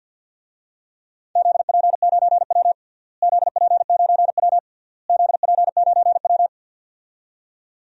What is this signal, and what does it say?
Morse code, heard as a keyed tone, about 52 wpm, 700 Hz. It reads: ZP9W ZP9W ZP9W
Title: DXCC-100 Call Signs - 4 Characters - Repeated 3x - 50wpm